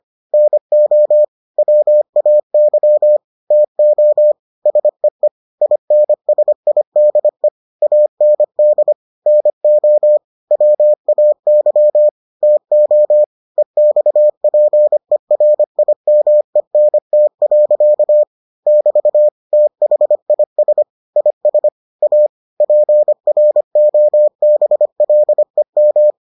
Morse code, heard as a keyed tone, about 25 wpm, 600 Hz. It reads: NO WAY TO SEE INSIDE AND NO WAY TO EXPERIMENT. <BT> THIS IS A PROBLEM